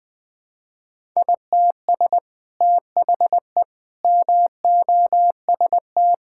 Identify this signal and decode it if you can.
Morse code, heard as a keyed tone, about 20 wpm, 700 Hz. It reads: ITS THE MOST
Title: Sets of 8 Words - 20wpm